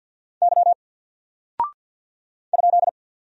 Morse code, heard as a keyed tone, about 48 wpm, 700 Hz. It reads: Y ?
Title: Single Character (Letter:Number:Punctuation:Prosign) Alt 50wpm